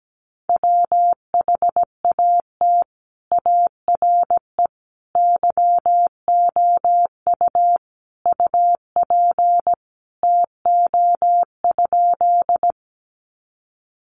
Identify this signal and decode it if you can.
Morse code, heard as a keyed tone, about 17 wpm, 700 Hz. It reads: WHAT ARE YOU UP TO?
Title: Sentences from Top 100 Words 17wpm